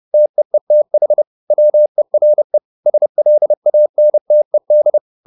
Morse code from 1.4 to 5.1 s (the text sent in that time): WERE SLANTED